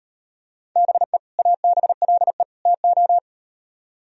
Morse code, heard as a keyed tone, about 38 wpm, 700 Hz. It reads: BE ABLE TO